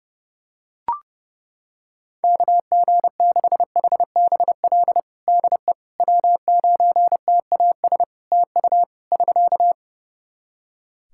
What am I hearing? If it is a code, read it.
Morse code, heard as a keyed tone, about 30 wpm, 700 Hz. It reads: KG6HBL DE W9TAS TU <SK>
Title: Practice QSO - Exchange 4 - Ending - 30wpm